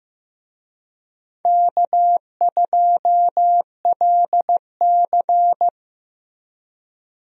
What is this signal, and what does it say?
Morse code, heard as a keyed tone, about 16 wpm, 700 Hz. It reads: K2LC
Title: USA Call Signs - 4 Characters - 15wpm